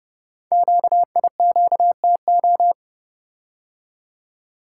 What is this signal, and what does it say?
Morse code, heard as a keyed tone, about 30 wpm, 690 Hz. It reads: QIQTO
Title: Code Groups - 5 Letters - 30wpm